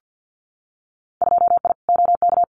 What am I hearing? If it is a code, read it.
Morse code, heard as a keyed tone, about 50 wpm, 700 Hz. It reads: 2I OK
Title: Winter Field Day Exchanges Encore 50wpm